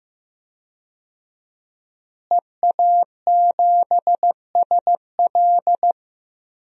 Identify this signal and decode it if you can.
Morse code, heard as a keyed tone, about 15 wpm, 700 Hz. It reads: EA7SL
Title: DXCC-100 Call Signs - 5 Characters - Repeated 3x - 15wpm